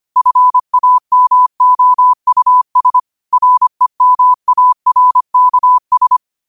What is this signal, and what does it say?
Morse code, heard as a keyed tone, about 25 wpm, 1000 Hz. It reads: FAMOUS REMARKS
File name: qotdfiles / 25wpm